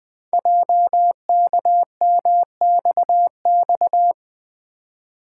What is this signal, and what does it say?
Morse code, heard as a keyed tone, about 20 wpm, 700 Hz. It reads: JKMXX
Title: Code Groups - 5 Letters - 20wpm